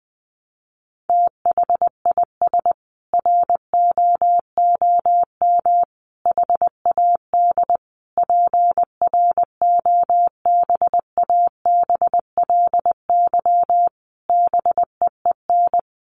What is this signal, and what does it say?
Morse code, heard as a keyed tone, about 20 wpm, 700 Hz. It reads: THIS ROOM HAD PROBABLY BEEN
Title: Sentences from Top 700 Words 20wpm